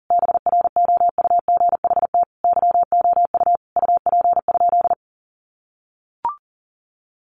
Morse code, heard as a keyed tone, about 40 wpm, 700 Hz. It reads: BROUGHT YOU UP?
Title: Sentences from Top 500 Words 40wpm